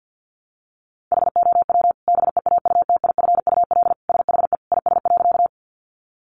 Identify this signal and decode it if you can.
Morse code, heard as a keyed tone, about 50 wpm, 700 Hz. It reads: HOW BEAUTIFUL SHE IS.
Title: Sentences from Top 900 Words 50wpm